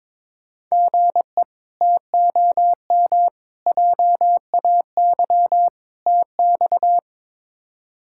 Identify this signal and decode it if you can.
Morse code, heard as a keyed tone, about 22 wpm, 700 Hz. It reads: GE TOM JAY TX
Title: K1USN SST QSO Practice - 22wpm